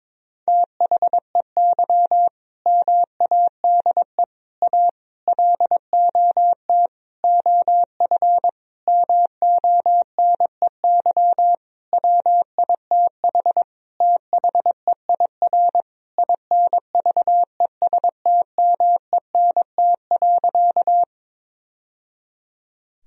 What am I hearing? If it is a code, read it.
Morse code, heard as a keyed tone, about 22 wpm, 700 Hz. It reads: THEY MADE A LOT OF MONEY WITH THEIR INVESTMENT.
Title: Sentences from Top 1000 Words - Review Part 4 - 22wpm